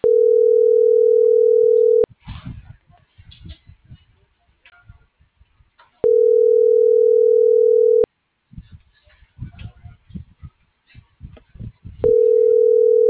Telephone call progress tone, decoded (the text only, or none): ringback tone